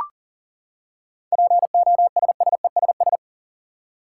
Morse code, heard as a keyed tone, about 40 wpm, 700 Hz. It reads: POSSESS
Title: Seven Letter Words - 40wpm